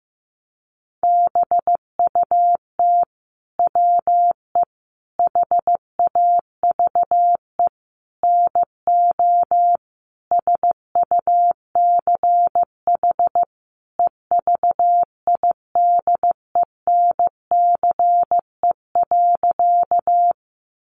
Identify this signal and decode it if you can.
Morse code, heard as a keyed tone, about 15 wpm, 700 Hz. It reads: BUT WE HAVE NO SUCH EVIDENCE.
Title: Sentences from Top 1300 Words 15wpm